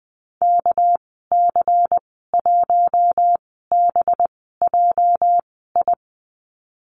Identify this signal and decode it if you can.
Morse code, heard as a keyed tone, about 20 wpm, spaced as for 15 wpm, 700 Hz. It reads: KC1BJI
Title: 231 Lesson 37g - Letter x - Call Signs 20wpm